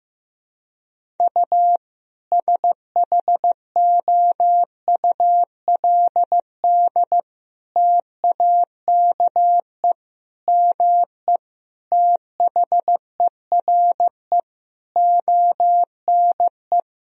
Morse code, heard as a keyed tone, about 15 wpm, 700 Hz. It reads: U SHOULD TAKE ME THERE ONE
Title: Sentences from Top 300 Words and Abbreviations 15wpm